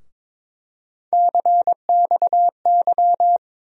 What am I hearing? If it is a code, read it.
Morse code, heard as a keyed tone, about 22 wpm, 700 Hz. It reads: CXY
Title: USA Call Sign Postfixes - 3 Letters 22wpm